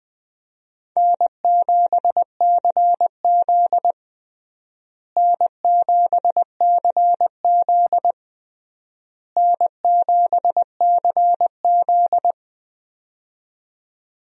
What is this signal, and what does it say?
Morse code, heard as a keyed tone, about 20 wpm, 700 Hz. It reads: N7CZ N7CZ N7CZ
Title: USA Call Signs - 4 Characters - Repeated 3x - 20wpm